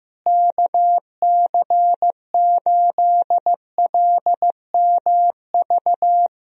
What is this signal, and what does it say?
Morse code, heard as a keyed tone, about 15 wpm, 700 Hz. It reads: KC8LMV
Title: USA Call Signs - 6 Characters - Repeated 2x - 15wpm